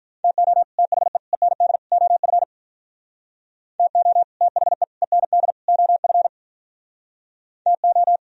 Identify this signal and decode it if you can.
Morse code, heard as a keyed tone, about 45 wpm, 700 Hz. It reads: TO THE END OF TO THE END OF TO